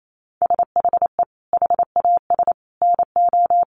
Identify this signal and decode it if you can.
Morse code, heard as a keyed tone, about 28 wpm, 700 Hz. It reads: SHE HAS NO
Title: Sentences from Top 1300 Words 28wpm